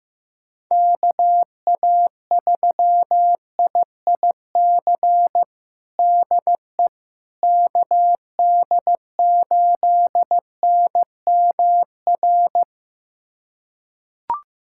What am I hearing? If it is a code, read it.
Morse code, heard as a keyed tone, about 15 wpm, 700 Hz. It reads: KA3IIC DE KD8NMR
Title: USA Callsign DE Callsign - 6 Characters - 15wpm